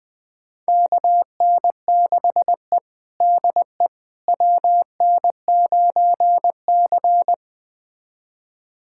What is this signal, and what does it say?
Morse code, heard as a keyed tone, about 21 wpm, 700 Hz. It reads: KN6E DE WN9C
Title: USA Callsign DE Callsign - 4 Characters - 20wpm